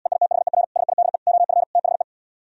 Morse code, heard as a keyed tone, about 77 wpm, 700 Hz. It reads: EITHER SIDE OF THE